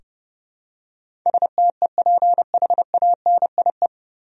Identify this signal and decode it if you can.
Morse code, heard as a keyed tone, about 32 wpm, 700 Hz. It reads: STEPHANIE
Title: USA Common Names - 30wpm